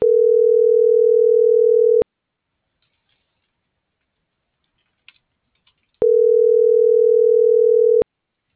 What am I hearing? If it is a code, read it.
Ringback tone (North America), 440 + 480 Hz.